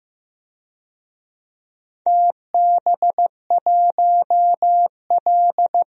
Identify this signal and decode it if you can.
Morse code, heard as a keyed tone, about 15 wpm, 700 Hz. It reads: TB1L
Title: DXCC-100 Call Signs - 4 Characters - 15wpm